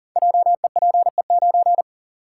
Morse code, heard as a keyed tone, about 40 wpm, 700 Hz. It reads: JEPE9
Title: Code Groups - 5 Letter:Numbers - 40wpm